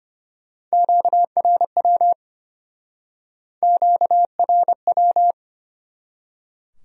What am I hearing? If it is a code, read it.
Morse code, heard as a keyed tone, about 33 wpm, 700 Hz. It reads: QRW QRW